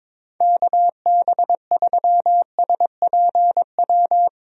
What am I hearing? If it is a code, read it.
Morse code, heard as a keyed tone, about 22 wpm, 700 Hz. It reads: KB3SPW
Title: USA Call Signs - 6 Characters - Repeated 3x - 22wpm